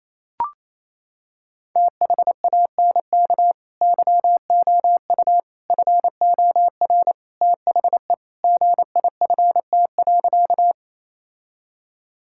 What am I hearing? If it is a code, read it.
Morse code, heard as a keyed tone, about 27 wpm, 700 Hz. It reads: THANK YOU FOR THE GIFT.
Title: Sentences from Top 1000 Words - Review Part 6 - 28wpm